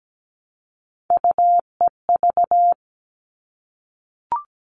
Morse code, heard as a keyed tone, about 17 wpm, 700 Hz. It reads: UEV
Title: All Cut Numbers - 3-Digit 17wpm